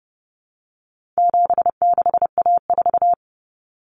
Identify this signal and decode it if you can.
Morse code, heard as a keyed tone, about 30 wpm, 700 Hz. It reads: 76A4
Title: Common Cut Numbers - 4-Digit 30wpm